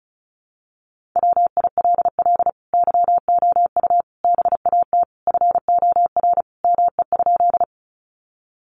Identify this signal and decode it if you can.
Morse code, heard as a keyed tone, about 34 wpm, 700 Hz. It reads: WILL YOU BAT FOR ME?
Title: Sentences from Top 1500 Words 35wpm